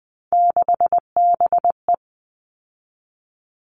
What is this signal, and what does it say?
Morse code, heard as a keyed tone, about 20 wpm, 700 Hz. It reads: 6BE
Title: All Cut Numbers - 3-Digit 20wpm